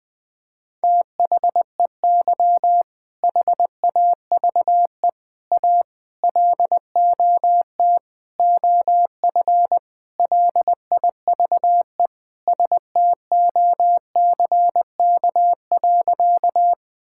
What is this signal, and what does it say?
Morse code, heard as a keyed tone, about 21 wpm, 700 Hz. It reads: THEY HAVE A LOT OF LIVE STOCK.
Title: Sentences from Top 1000 Words - Review Part 5 - 20wpm